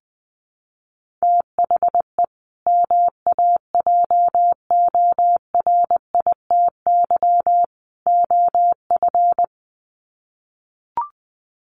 Morse code, heard as a keyed tone, about 20 wpm, 700 Hz. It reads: THE MAJORITY OF
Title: Sets of 3 Words 20wpm